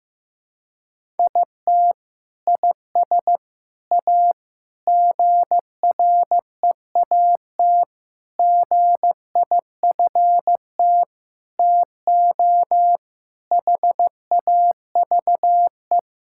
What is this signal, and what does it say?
Morse code, heard as a keyed tone, about 15 wpm, 700 Hz. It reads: IT IS A GREAT GIFT TO HAVE
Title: Sentences from Top 1000 Words - Review Part 6 - 15wpm